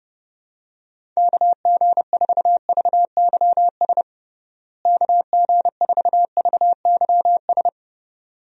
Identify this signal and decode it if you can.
Morse code, heard as a keyed tone, about 31 wpm, 700 Hz. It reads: KG4VYS KG4VYS
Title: USA Call Signs - 6 Characters - Repeated 2x - 30wpm